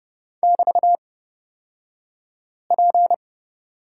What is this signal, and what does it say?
Morse code, heard as a keyed tone, about 30 wpm, spaced as for 5 wpm, 690 Hz. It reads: <BT>P
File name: Lesson 36d - BT - Two Characters 30wpm